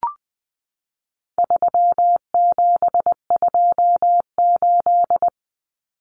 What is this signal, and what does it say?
Morse code, heard as a keyed tone, about 20 wpm, 700 Hz. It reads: 3728
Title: Common Cut Numbers - 4-Digit 20wpm